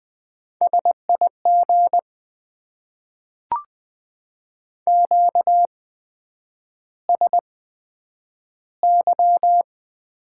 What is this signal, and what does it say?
Morse code, heard as a keyed tone, about 20 wpm, spaced as for 9 wpm, 700 Hz. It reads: SIG QSY
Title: Top 90 QSO Elements - 8x Character Spacing - 20wpm